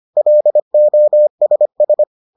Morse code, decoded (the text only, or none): LOSS